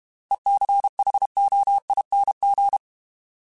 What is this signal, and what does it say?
Morse code, heard as a keyed tone, about 32 wpm, 800 Hz. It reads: ECHOING